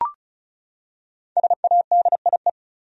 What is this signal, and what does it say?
Morse code, heard as a keyed tone, about 36 wpm, 700 Hz. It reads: SADIE
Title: USA Common Names - 35wpm